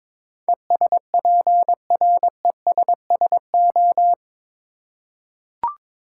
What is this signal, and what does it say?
Morse code, heard as a keyed tone, about 22 wpm, 700 Hz. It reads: ESPRESSO
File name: practice / 22wpm / Difficult Words with Hints - 22wpm